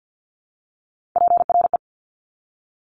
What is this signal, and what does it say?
Morse code, heard as a keyed tone, about 50 wpm, 700 Hz. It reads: PRE
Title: Top Three Letter Combinations in Words 50wpm